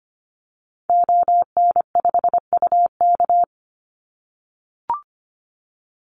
Morse code, heard as a keyed tone, about 25 wpm, 700 Hz. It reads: ON5UK
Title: Call Signs - 25wpm